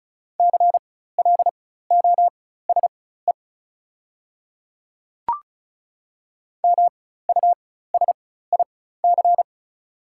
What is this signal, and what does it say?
Morse code, heard as a keyed tone, about 34 wpm, spaced as for 16 wpm, 700 Hz. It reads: CLOSE MUSIC
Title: 4x Character Spacing - Top 500 Words - 35wpm